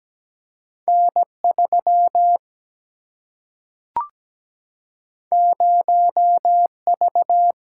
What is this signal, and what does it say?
Morse code, heard as a keyed tone, about 17 wpm, 700 Hz. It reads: N 3 0 V
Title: Single-Letter Single-Number 17wpm